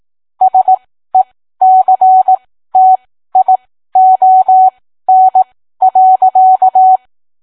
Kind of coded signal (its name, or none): Morse code